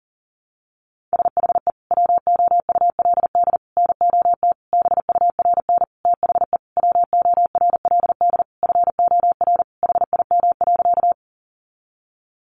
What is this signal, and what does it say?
Morse code, heard as a keyed tone, about 40 wpm, 700 Hz. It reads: SHE WOULD NOT BURN THE WORLD FOR HIM.
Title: Sentences from Top 1300 Words 40wpm